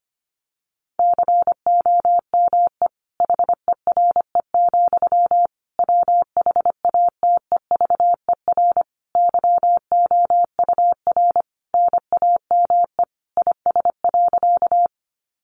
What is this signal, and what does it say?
Morse code, heard as a keyed tone, about 25 wpm, 700 Hz. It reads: COME HERE, WHATEVER YOUR NAME IS.
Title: Sentences from Top 1100 Words 25wpm